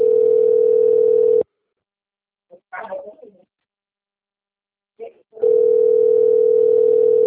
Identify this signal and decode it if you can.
Ringback tone (North America), 440 + 480 Hz.